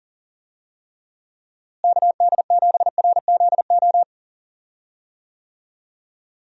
Morse code, heard as a keyed tone, about 40 wpm, 700 Hz. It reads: KD7RZO
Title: USA Call Signs - 6 Characters - Repeated 3x - 40wpm